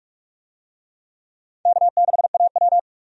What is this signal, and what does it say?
Morse code, heard as a keyed tone, about 47 wpm, 700 Hz. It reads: K6AW
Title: USA Call Signs - 4 Characters - 45wpm